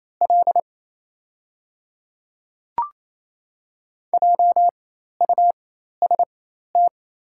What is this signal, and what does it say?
Morse code, heard as a keyed tone, about 28 wpm, spaced as for 13 wpm, 700 Hz. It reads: L JUST
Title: 4x Character Spacing - Top 500 Words - 28wpm